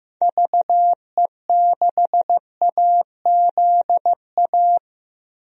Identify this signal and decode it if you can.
Morse code, heard as a keyed tone, about 15 wpm, 700 Hz. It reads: VE6AZA
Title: DXCC-100 Call Signs - 6 Characters - Repeated 3x - 15wpm